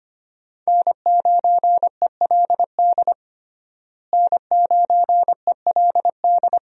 Morse code, heard as a keyed tone, about 25 wpm, 700 Hz. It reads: N9ELD N9ELD
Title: USA Call Signs - 5 Characters - Repeated 2x - 25wpm